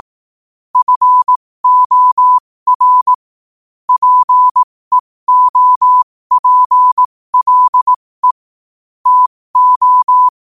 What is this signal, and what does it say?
Morse code, heard as a keyed tone, about 18 wpm, spaced as for 15 wpm, 1000 Hz. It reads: FOR PEOPLE TO
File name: 15wpm